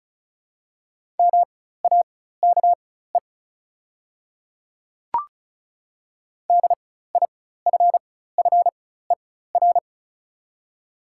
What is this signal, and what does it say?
Morse code, heard as a keyed tone, about 36 wpm, spaced as for 16 wpm, 700 Hz. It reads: MAKE DIFFER
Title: 4x Character Spacing - Top 300 Words - 35wpm